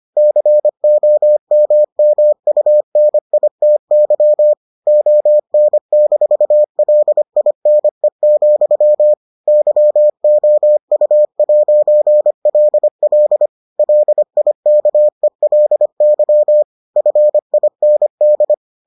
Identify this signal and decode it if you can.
Morse code, heard as a keyed tone, about 25 wpm, 600 Hz. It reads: COMMUNITY ON-LINE, YOU'LL LIKELY FIND